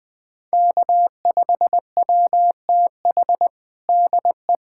Morse code, heard as a keyed tone, about 19 wpm, 700 Hz. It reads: K5WTH DE
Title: USA Callsign DE Callsign - 5 Characters - 20wpm